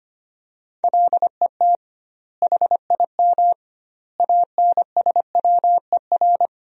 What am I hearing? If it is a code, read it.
Morse code, heard as a keyed tone, about 25 wpm, 700 Hz. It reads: LET HIM ANSWER
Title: 2x Word-Spacing - Sentences from Top 300 Words - 25wpm